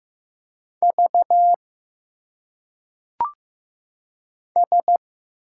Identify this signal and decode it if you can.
Morse code, heard as a keyed tone, about 15 wpm, 700 Hz. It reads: V S
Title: Single Letters - Phonetic Pronunciation 15wpm